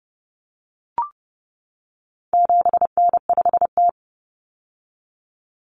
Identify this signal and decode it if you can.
Morse code, heard as a keyed tone, about 30 wpm, 700 Hz. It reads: 7N5T
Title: Common Cut Numbers - 4-Digit 30wpm